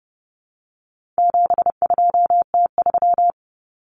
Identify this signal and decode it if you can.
Morse code, heard as a keyed tone, about 30 wpm, 700 Hz. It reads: 72T3
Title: Common Cut Numbers - 4-Digit 30wpm